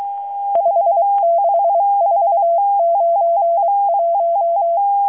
Morse code, heard as a keyed tone, about 23 wpm, spaced as for 21 wpm, 680 Hz. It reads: 56491
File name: HM02- -0437utc-7351khz